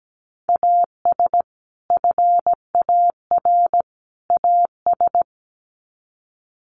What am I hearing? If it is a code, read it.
Morse code, heard as a keyed tone, about 17 wpm, 700 Hz. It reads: AS FAR AS
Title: Sets of 3 Words 17wpm